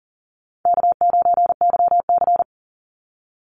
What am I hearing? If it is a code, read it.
Morse code, heard as a keyed tone, about 40 wpm, 700 Hz. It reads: K9YC
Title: Easy Call Signs - 40wpm